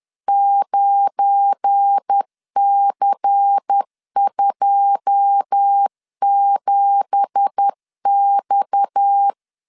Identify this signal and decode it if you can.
Morse code, heard as a keyed tone, about 10 wpm, 800 Hz. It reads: 9C27X
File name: CW / DET / 1 9 1500C S B 10wpm 4439